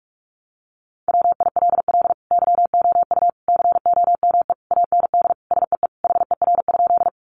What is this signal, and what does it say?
Morse code, heard as a keyed tone, about 44 wpm, 700 Hz. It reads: WILL YOU COME AND SEE HER?
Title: Sentences from Top 100 Words 45wpm